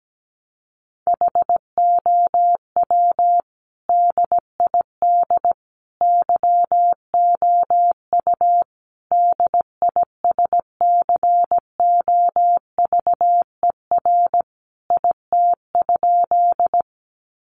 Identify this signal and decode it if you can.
Morse code, heard as a keyed tone, about 17 wpm, 700 Hz. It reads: HOW DID YOU DISCOVER IT?
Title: Sentences from Top 1100 Words 17wpm